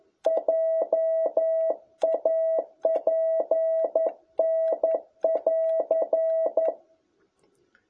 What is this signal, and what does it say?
Morse code, heard as a keyed tone, about 11 wpm, 660 Hz. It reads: JA PN <AR>
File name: jh8cbh / 2021cw